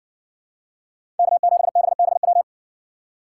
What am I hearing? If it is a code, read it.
Morse code, heard as a keyed tone, about 57 wpm, 700 Hz. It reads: K7BBW